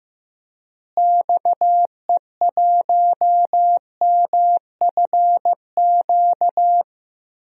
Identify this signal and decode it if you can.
Morse code, heard as a keyed tone, about 15 wpm, 700 Hz. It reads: XE1MFQ
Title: DXCC-100 Call Signs - 6 Characters - Repeated 2x - 15wpm